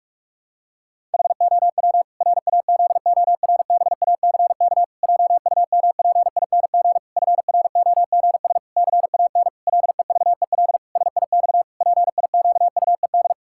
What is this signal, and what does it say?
Morse code, heard as a keyed tone, about 41 wpm, 700 Hz. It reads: HOW RAZORBACK JUMPING FROGS CAN LEVEL SIX PIQUED
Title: Pangram Challenge v2 - 45wpm